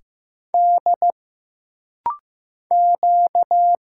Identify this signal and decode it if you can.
Morse code, heard as a keyed tone, about 15 wpm, 700 Hz. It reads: D Q
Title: Single Letters - Rapid-Fire 15wpm